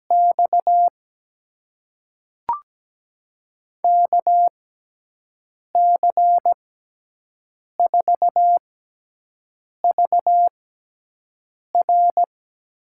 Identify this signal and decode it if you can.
Morse code, heard as a keyed tone, about 17 wpm, spaced as for 6 wpm, 700 Hz. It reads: X KC4VR
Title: USA Call Signs - 5 Characters - 6x Character Spacing - 17wpm